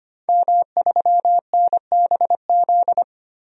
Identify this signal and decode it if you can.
Morse code, heard as a keyed tone, about 25 wpm, 700 Hz. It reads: M3NBZ
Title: DXCC-100 Call Signs - 5 Characters - Repeated 2x - 25wpm